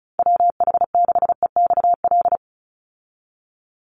Morse code, heard as a keyed tone, about 34 wpm, 700 Hz. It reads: WH6EXL
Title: USA Call Signs 35wpm